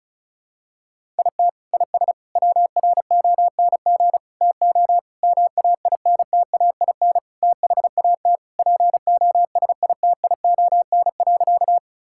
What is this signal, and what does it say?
Morse code, heard as a keyed tone, about 34 wpm, 700 Hz. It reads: IT IS WRONG TO MAINTAIN THAT POSITION.
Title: Sentences from Top 1000 Words - Review Part 5 - 35wpm